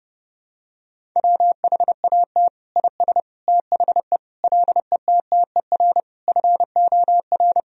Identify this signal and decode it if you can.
Morse code, heard as a keyed tone, about 30 wpm, 700 Hz. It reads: WHAT IS THE LETTER FOR
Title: Sentences from Top 1000 Words - Review Part 2 - 30wpm